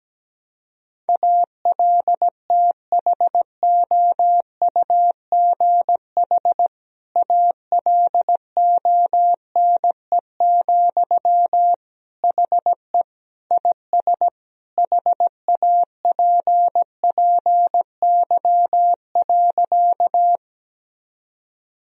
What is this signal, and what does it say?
Morse code, heard as a keyed tone, about 17 wpm, 700 Hz. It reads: ALTHOUGH ALONE, HE IS HAPPY.
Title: Sentences from Top 1000 Words - Review Part 3 - 17wpm